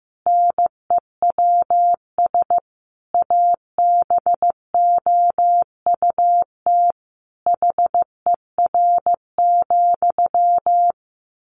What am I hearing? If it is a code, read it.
Morse code, heard as a keyed tone, about 14 wpm, 700 Hz. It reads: NEWS ABOUT HER,
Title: Sentences from Top 1500 Words 15wpm